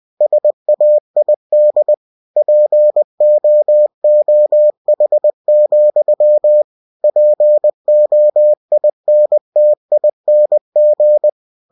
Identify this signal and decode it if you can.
Morse code, heard as a keyed tone, about 20 wpm, 600 Hz. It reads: SAID POOH, POINTING
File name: cw-books / winnie-the-pooh / winnie-20wpm-page-0099